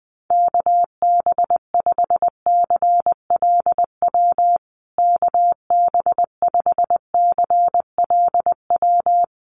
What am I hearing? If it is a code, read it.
Morse code, heard as a keyed tone, about 20 wpm, 700 Hz. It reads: KB5CLW KB5CLW
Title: USA Call Signs Repeated 2x 20wpm